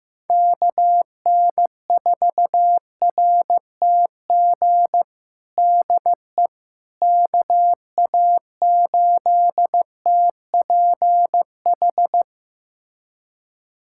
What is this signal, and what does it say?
Morse code, heard as a keyed tone, about 15 wpm, 700 Hz. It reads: KN4RTG DE KA8TPH
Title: USA Callsign DE Callsign - 6 Characters - 15wpm